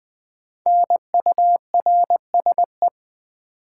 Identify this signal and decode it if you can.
Morse code, heard as a keyed tone, about 20 wpm, 700 Hz. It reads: NURSE
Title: Five Letter Words - 20wpm